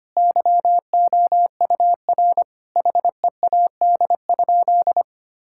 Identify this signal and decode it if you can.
Morse code, heard as a keyed tone, about 25 wpm, 700 Hz. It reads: YOUR HEAD?
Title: Sentences from Top 1000 Words - Review Part 5 - 25wpm